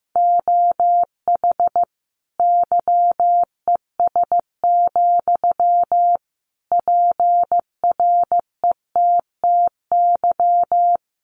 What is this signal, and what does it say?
Morse code, heard as a keyed tone, about 15 wpm, 700 Hz. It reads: OH YES, PRETTY
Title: Sentences from Top 1000 Words 15wpm